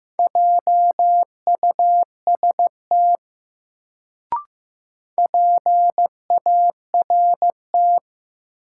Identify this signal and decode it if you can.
Morse code, heard as a keyed tone, about 15 wpm, 700 Hz. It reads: JUST PART
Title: Sentences from Top 200 Words - New Word Drill - 15wpm